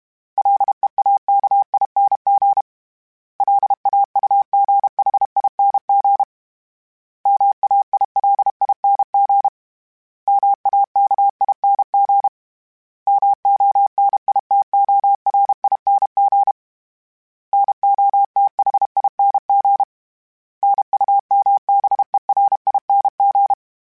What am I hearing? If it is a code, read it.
Morse code, heard as a keyed tone, about 32 wpm, 800 Hz. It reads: LEAKING LAUGHING MAILING MAKING MONITORING NOTHING NUMBERING
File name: Special INGs-mp3s / Special INGs-mp3s / Sngl Ing @30